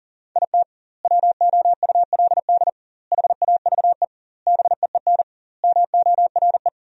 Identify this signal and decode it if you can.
Morse code, heard as a keyed tone, about 40 wpm, 700 Hz. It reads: IT WOULD HAVE BEEN MORE
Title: 2x Word-Spacing - Sentences from Top 400 Words - 40wpm